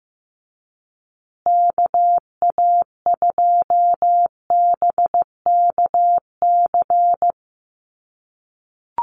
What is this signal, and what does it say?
Morse code, heard as a keyed tone, about 15 wpm, 700 Hz. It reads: KA2BKC
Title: USA Call Signs Repeated 3x 15wpm